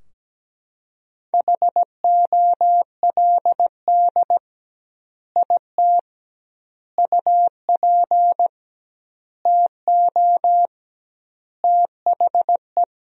Morse code, heard as a keyed tone, about 17 wpm, 700 Hz. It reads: HOLD IT UP TO THE
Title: 2x Word-Spacing - Sentences from Top 400 Words - 17wpm